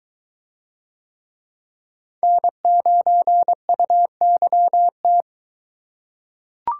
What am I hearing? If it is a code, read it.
Morse code, heard as a keyed tone, about 22 wpm, 700 Hz. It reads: N9UYT